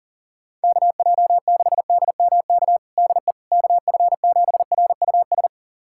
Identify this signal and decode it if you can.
Morse code, heard as a keyed tone, about 37 wpm, 700 Hz. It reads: KJ6DMK DE KF7RUS
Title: USA Callsign DE Callsign - 6 Characters - 40wpm